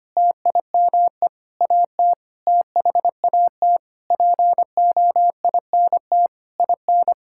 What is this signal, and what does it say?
Morse code, heard as a keyed tone, about 25 wpm, 700 Hz. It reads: TIME AT THAT POINT IN
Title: Sentences from Top 1000 Words - Review Part 5 - 25wpm